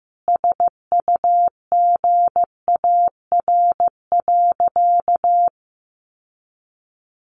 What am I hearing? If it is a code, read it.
Morse code, heard as a keyed tone, about 15 wpm, 700 Hz. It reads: SUGAR.
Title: Sentences from Top 1200 Words 15wpm